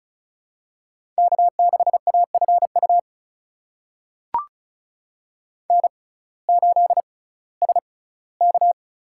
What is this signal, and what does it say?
Morse code, heard as a keyed tone, about 36 wpm, spaced as for 19 wpm, 700 Hz. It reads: K6AFU N8SK
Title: USA Call Signs - 5 Characters - 6x Character Spacing - 35wpm